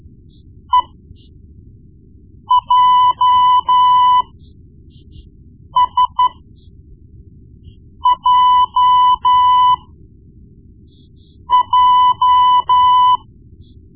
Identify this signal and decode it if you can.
Morse code, heard as a keyed tone, about 12 wpm, spaced as for 4 wpm, 1000 Hz. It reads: EJSJJ
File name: Morse Lessons / Morse Lessons / Tape-1B